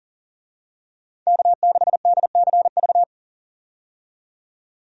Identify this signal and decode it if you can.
Morse code, heard as a keyed tone, about 40 wpm, 700 Hz. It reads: K6DCV
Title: USA Call Signs - 5 Characters - 40wpm